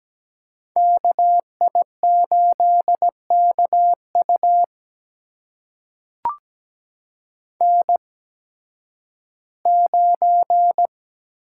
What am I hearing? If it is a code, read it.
Morse code, heard as a keyed tone, about 17 wpm, spaced as for 11 wpm, 700 Hz. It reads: KI8KU N9
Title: USA Call Signs - 8x Character Spacing - 17wpm